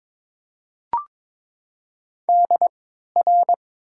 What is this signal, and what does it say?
Morse code, heard as a keyed tone, about 22 wpm, 700 Hz. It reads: D R
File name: Top 90 QSO Elements - 3x Character Spacing - 22wpm